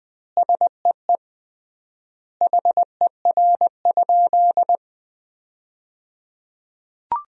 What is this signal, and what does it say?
Morse code, heard as a keyed tone, about 20 wpm, 700 Hz. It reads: SEE HER?
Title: 3x Word-Spacing - Sentences from Top 100 Words - 20wpm